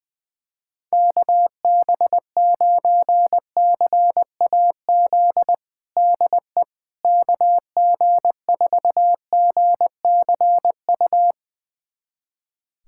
Morse code, heard as a keyed tone, about 20 wpm, 700 Hz. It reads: KB9CAZ DE KG4GCU
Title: USA Callsign DE Callsign 20wpm